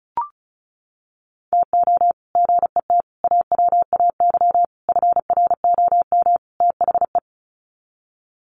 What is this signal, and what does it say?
Morse code, heard as a keyed tone, about 34 wpm, 700 Hz. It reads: TO GET AWAY FROM THE
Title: Sets of 5 Words Encore2 35wpm